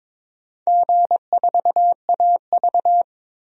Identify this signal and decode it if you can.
Morse code, heard as a keyed tone, about 21 wpm, 700 Hz. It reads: G4AV
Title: DXCC-100 Call Signs - 4 Characters - 22wpm